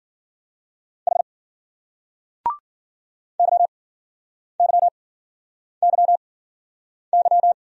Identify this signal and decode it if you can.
Morse code, heard as a keyed tone, about 51 wpm, spaced as for 9 wpm, 700 Hz. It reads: H YYYY